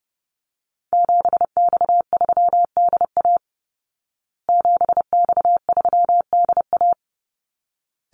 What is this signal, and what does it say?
Morse code, heard as a keyed tone, about 30 wpm, 700 Hz. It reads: 7X3DA 7X3DA
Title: Call Signs Repeated 3x - 30wpm